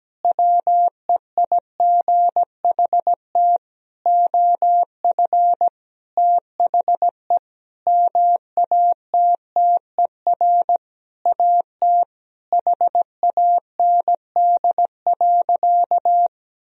Morse code, heard as a keyed tone, about 17 wpm, 700 Hz. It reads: WEIGHT OF THE MATTER AT HAND.
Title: Sentences from Top 1000 Words - Review Part 3 - 17wpm